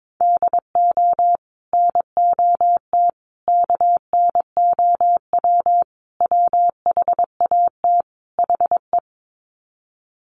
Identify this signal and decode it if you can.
Morse code, heard as a keyed tone, about 22 wpm, 700 Hz. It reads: DO NOT KNOW WHAT HE
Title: Sets of 5 Words Encore2 22wpm